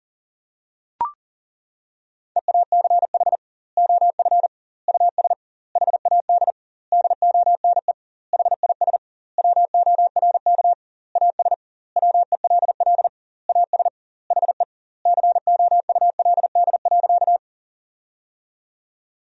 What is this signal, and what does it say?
Morse code, heard as a keyed tone, about 40 wpm, 700 Hz. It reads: EACH OF US HAD DONE HIS WORK AS WELL AS HE COULD.
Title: 2x Word-Spacing - Sentences from Top 500 Words - 40wpm